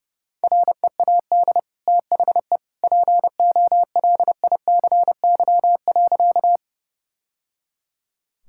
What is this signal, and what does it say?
Morse code, heard as a keyed tone, about 30 wpm, 700 Hz. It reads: READ THE POLICY.
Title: Sentences from Top 1000 Words - Review Part 5 - 30wpm